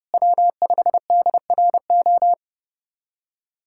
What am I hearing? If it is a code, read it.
Morse code, heard as a keyed tone, about 30 wpm, 700 Hz. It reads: W5DRO
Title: USA Callsign DE Callsign - 5 Characters - 30wpm